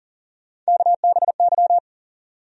Morse code, heard as a keyed tone, about 40 wpm, 700 Hz. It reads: KBY
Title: USA Call Sign Postfixes - 3 Letters 40wpm